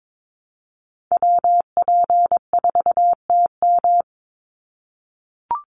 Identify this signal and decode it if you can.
Morse code, heard as a keyed tone, about 22 wpm, 700 Hz. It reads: WP4TM
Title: USA Call Signs 22wpm